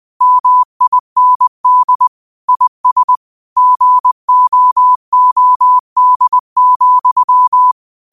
Morse code, heard as a keyed tone, about 20 wpm, 1000 Hz. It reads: MIND IS GOOD,
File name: qotdfiles / 20wpm